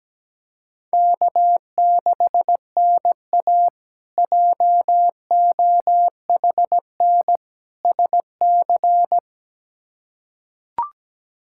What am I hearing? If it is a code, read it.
Morse code, heard as a keyed tone, about 17 wpm, 700 Hz. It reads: K6NA JOHN SC
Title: K1USN SST QSO Practice - 17wpm